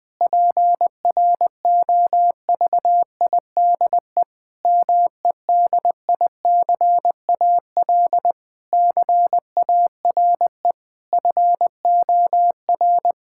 Morse code, heard as a keyed tone, about 20 wpm, 700 Hz. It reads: PROVIDE MEDICAL CARE FOR
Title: Sentences from Top 1000 Words - Review Part 5 - 20wpm